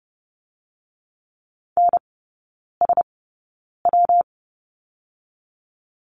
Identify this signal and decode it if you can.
Morse code, heard as a keyed tone, about 30 wpm, 700 Hz. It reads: N S W
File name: Lesson 19e - Letter w - Three Characters 30wpm